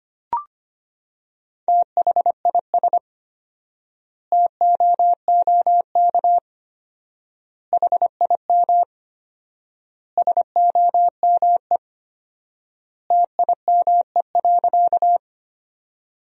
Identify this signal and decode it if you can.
Morse code, heard as a keyed tone, about 25 wpm, 700 Hz. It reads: THIS TOOK HIM SOME TIME.
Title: 4x Word-Spacing - Sentences from Top 300 Words - 25wpm